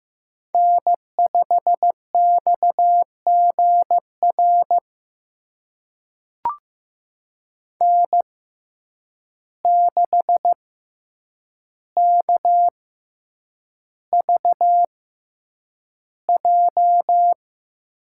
Morse code, heard as a keyed tone, about 15 wpm, spaced as for 8 wpm, 700 Hz. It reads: N5XGR N6KVJ
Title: USA Call Signs - 5 Characters - 6x Character Spacing - 15wpm